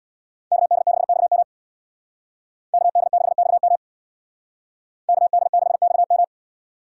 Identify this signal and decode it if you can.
Morse code, heard as a keyed tone, about 64 wpm, 690 Hz. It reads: KD6XM KD6XM KD6XM